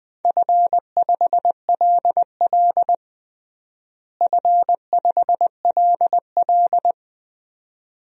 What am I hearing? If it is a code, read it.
Morse code, heard as a keyed tone, about 20 wpm, 700 Hz. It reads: F5LL F5LL
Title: DXCC-100 Call Signs - 4 Characters - Repeated 2x - 20wpm